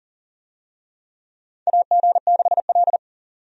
Morse code, heard as a keyed tone, about 40 wpm, 700 Hz. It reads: AG6L
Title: USA Call Signs - 4 Characters - 40wpm